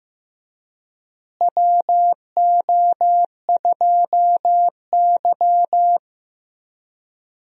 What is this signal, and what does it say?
Morse code, heard as a keyed tone, about 15 wpm, 700 Hz. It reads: WO2Y
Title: USA Call Signs - 4 Characters - Repeated 3x - 15wpm